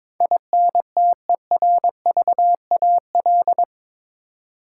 Morse code, heard as a keyed tone, about 21 wpm, 700 Hz. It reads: INTERVAL
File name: practice / 22wpm / Eight Letter Words - 22wpm